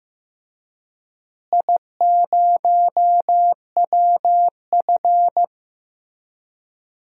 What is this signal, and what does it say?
Morse code, heard as a keyed tone, about 15 wpm, 700 Hz. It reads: I0WF
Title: DXCC-100 Call Signs - 4 Characters - Repeated 2x - 15wpm